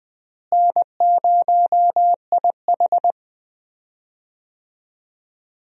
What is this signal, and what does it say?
Morse code, heard as a keyed tone, about 20 wpm, 700 Hz. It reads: N0IH
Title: USA Call Signs - 4 Characters - Repeated 2x - 20wpm